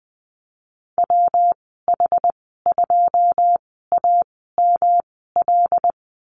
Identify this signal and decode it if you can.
Morse code, heard as a keyed tone, about 20 wpm, spaced as for 15 wpm, 700 Hz. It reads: WH2AML
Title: 122 Lesson 20g - Number 3 - Call Signs 20wpm